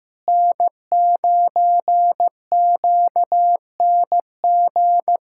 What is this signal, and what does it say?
Morse code, heard as a keyed tone, about 15 wpm, 700 Hz. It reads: N9QNG
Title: USA Callsign DE Callsign 15wpm